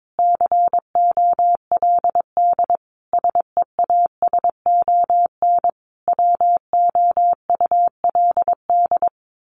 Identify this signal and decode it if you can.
Morse code, heard as a keyed tone, about 22 wpm, 700 Hz. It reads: COLD SEASON WOULD
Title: Sentences from Top 1400 Words 22wpm